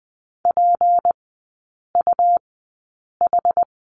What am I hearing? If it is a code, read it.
Morse code, heard as a keyed tone, about 20 wpm, spaced as for 8 wpm, 700 Hz. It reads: PUH
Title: 152 Lesson 25e - Letter p - Three Characters 20wpm